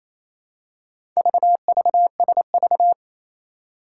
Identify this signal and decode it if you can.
Morse code, heard as a keyed tone, about 28 wpm, 700 Hz. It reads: VVSV
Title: Four Challenging Letter-Number Combinations 28wpm